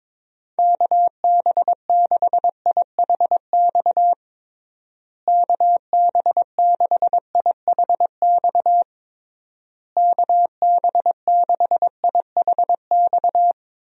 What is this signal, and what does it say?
Morse code, heard as a keyed tone, about 22 wpm, 700 Hz. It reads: KB6IHX KB6IHX KB6IHX
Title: USA Call Signs - 6 Characters - Repeated 3x - 22wpm